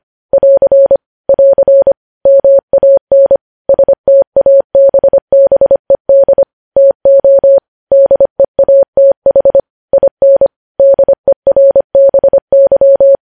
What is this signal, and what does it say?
Morse code, heard as a keyed tone, about 25 wpm, 550 Hz. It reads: <AR> <AR> MAN STABBED TO DEATH IN DERBY
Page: news headlines in morse code at 25 wpm